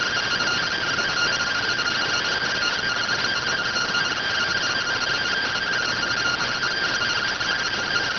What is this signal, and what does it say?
Two-tone teleprinter signal, about 170 Hz shift.